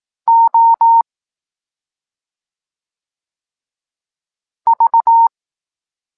Morse code, heard as a keyed tone, about 18 wpm, 920 Hz. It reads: O V
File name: Tekst-26 1